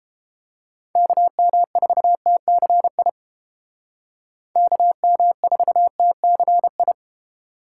Morse code, heard as a keyed tone, about 29 wpm, 700 Hz. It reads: KM4TCI KM4TCI